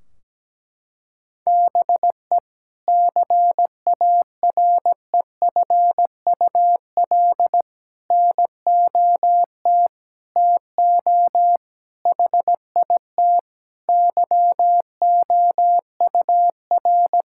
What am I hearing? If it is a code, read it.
Morse code, heard as a keyed tone, about 17 wpm, 700 Hz. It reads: BE CAREFUL NOT TO HIT YOUR
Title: Sentences from Top 1600 Words 17wpm